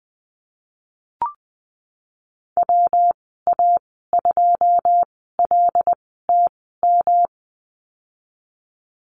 Morse code, heard as a keyed tone, about 19 wpm, spaced as for 14 wpm, 700 Hz. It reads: WA2LTM
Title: 140 Lesson 23g - Letter f - Call Signs 20wpm